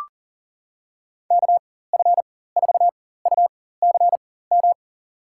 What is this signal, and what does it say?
Morse code, heard as a keyed tone, about 38 wpm, spaced as for 18 wpm, 700 Hz. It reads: KF4UCM
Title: USA Call Signs - 4x Character Spacing - 40wpm